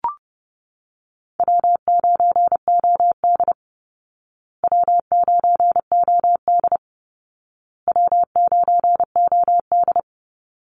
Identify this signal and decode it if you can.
Morse code, heard as a keyed tone, about 30 wpm, 700 Hz. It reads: W9OD W9OD W9OD
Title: Call Signs Repeated 2x - 30wpm